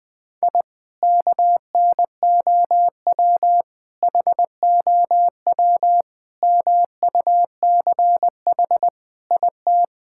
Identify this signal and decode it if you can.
Morse code, heard as a keyed tone, about 20 wpm, 700 Hz. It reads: I KNOW HOW MUCH IT
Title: Sentences from Top 1000 Words - Review Part 5 - 20wpm